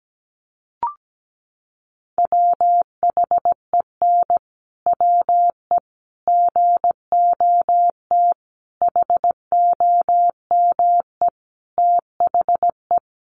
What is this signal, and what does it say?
Morse code, heard as a keyed tone, about 17 wpm, 700 Hz. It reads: WHEN WE GOT HOME THE
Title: Sentences from Top 500 Words 17wpm